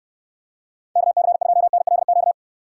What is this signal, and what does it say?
Morse code, heard as a keyed tone, about 69 wpm, 700 Hz. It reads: KC2NLQ